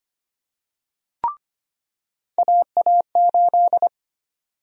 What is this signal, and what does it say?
Morse code, heard as a keyed tone, about 25 wpm, 700 Hz. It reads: AA8
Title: USA Call Sign Prefixes - 25wpm